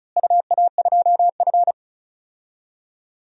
Morse code, heard as a keyed tone, about 36 wpm, 700 Hz. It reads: UA2F
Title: DXCC-100 Call Signs - 4 Characters - Repeated 3x - 35wpm